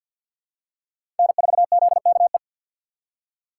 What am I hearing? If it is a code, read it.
Morse code, heard as a keyed tone, about 50 wpm, 700 Hz. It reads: N4ZKE
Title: USA Call Signs - 5 Characters - 2x Character Spacing - 50wpm